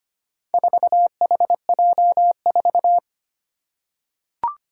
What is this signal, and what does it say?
Morse code, heard as a keyed tone, about 25 wpm, 700 Hz. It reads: 4HJ4
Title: Four Challenging Letter-Number Combinations 25wpm